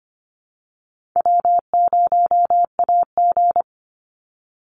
Morse code, heard as a keyed tone, about 25 wpm, 700 Hz. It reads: W0AG
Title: Easy Call Signs Repeated 3x - 25wpm